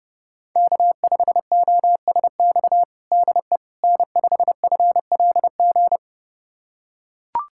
Morse code, heard as a keyed tone, about 32 wpm, 700 Hz. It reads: K5OSX DE N5FLG
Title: USA Callsign DE Callsign - 5 Characters - 30wpm